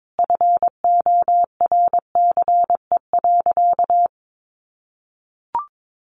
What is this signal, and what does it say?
Morse code, heard as a keyed tone, about 22 wpm, 700 Hz. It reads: FORCE.
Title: Sentences from Top 700 Words 22wpm